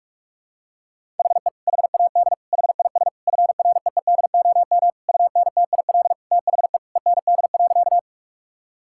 Morse code, heard as a keyed tone, about 46 wpm, 700 Hz. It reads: HE HAD HIS FREEDOM UNTIL THE END.
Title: Sentences from Top 1000 Words - Review Part 5 - 45wpm